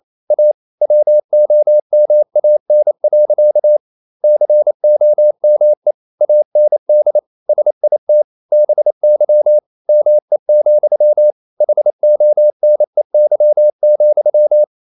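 Morse code, heard as a keyed tone, about 28 wpm, 600 Hz. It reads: A WOMAN. COME AND SIT BY ME, HONEY,
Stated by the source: Morse code practice at 28 wpm, text of Ken Follett's novel Never in English, from CW med Gnister.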